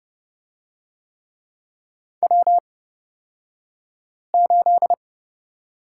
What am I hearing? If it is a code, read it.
Morse code, heard as a keyed tone, about 30 wpm, spaced as for 5 wpm, 700 Hz. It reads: W8
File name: Lesson 37d - Letter x - Two Characters 30wpm